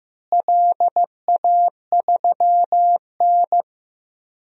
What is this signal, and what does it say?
Morse code, heard as a keyed tone, about 15 wpm, 700 Hz. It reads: LA3N
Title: DXCC-100 Call Signs - 4 Characters - Repeated 2x - 15wpm